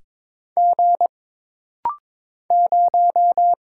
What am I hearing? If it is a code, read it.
Morse code, heard as a keyed tone, about 22 wpm, 700 Hz. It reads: G 0
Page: This rapid-fire timing is geared to push your ICR (Instant Character Recognition) skills.